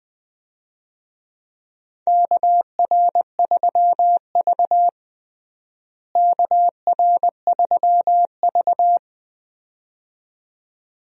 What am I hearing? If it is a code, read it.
Morse code, heard as a keyed tone, about 20 wpm, 700 Hz. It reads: KR3V KR3V
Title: USA Call Signs - 4 Characters - Repeated 2x - 20wpm